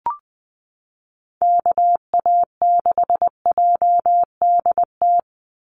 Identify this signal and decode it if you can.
Morse code, heard as a keyed tone, about 20 wpm, 700 Hz. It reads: KA6JDT
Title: USA Call Signs 20wpm